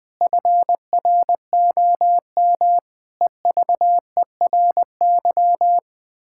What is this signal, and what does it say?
Morse code, heard as a keyed tone, about 20 wpm, 700 Hz. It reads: FROM EVERY
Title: Pangram Challenge v2 - 20wpm